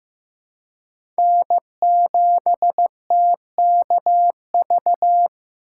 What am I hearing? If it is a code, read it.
Morse code, heard as a keyed tone, about 15 wpm, 700 Hz. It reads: N7TKV
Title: USA Call Signs - 4x Character Spacing - 15wpm